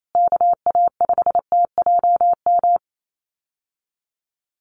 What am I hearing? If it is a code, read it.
Morse code, heard as a keyed tone, about 28 wpm, 700 Hz. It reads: KA5TJM
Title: USA Call Signs Repeated 3x 28wpm